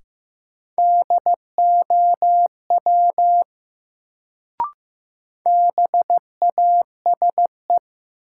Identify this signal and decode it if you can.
Morse code, heard as a keyed tone, about 15 wpm, 700 Hz. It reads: DOW BASE
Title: Sets of 5 Words - New Word Drill - 15wpm